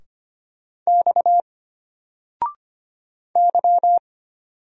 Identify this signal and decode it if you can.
Morse code, heard as a keyed tone, about 25 wpm, 700 Hz. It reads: X Y
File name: Single-Letter-Number-25wpm